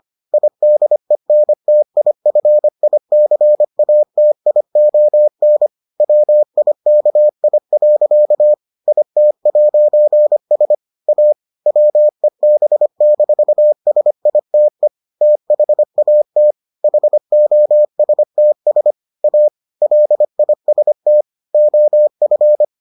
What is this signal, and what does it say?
Morse code, heard as a keyed tone, about 25 wpm, 600 Hz. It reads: IDENTIFICATION WIKI. IT'S A WEB-SITE THAT HOSTS A LIST OF